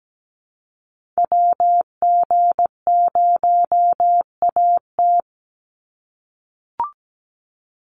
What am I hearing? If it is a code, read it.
Morse code, heard as a keyed tone, about 17 wpm, 700 Hz. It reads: WG0AT
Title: NA SOTA Activator Callsigns 17wpm